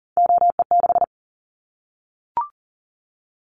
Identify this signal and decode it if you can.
Morse code, heard as a keyed tone, about 38 wpm, 700 Hz. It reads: OE6
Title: Callsign Prefixes for Top 100 DXCC - 40wpm